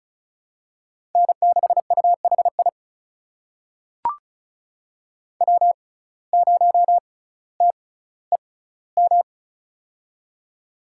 Morse code, heard as a keyed tone, about 36 wpm, spaced as for 18 wpm, 700 Hz. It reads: N6UHI W0TEM
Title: USA Call Signs - 5 Characters - 6x Character Spacing - 35wpm